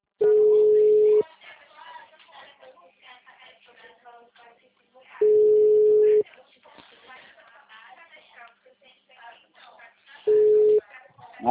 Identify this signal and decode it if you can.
Ringback tone (Europe, 425 Hz), with speech.
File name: out- -2002- - - .21276